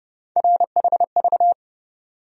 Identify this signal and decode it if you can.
Morse code, heard as a keyed tone, about 32 wpm, 700 Hz. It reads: RHV
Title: USA Call Sign Postfixes - 3 Letters 30wpm